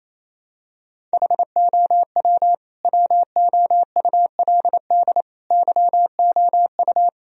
Morse code, heard as a keyed tone, about 27 wpm, 700 Hz. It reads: HOW WOULD YOU
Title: Sentences from Top 1000 Words - Review Part 1 - 28wpm